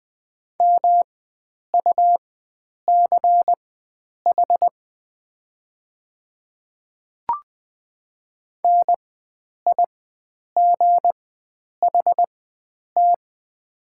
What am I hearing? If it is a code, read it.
Morse code, heard as a keyed tone, about 20 wpm, spaced as for 9 wpm, 700 Hz. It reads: MUCH NIGHT
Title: 4x Character Spacing - Top 300 Words - 20wpm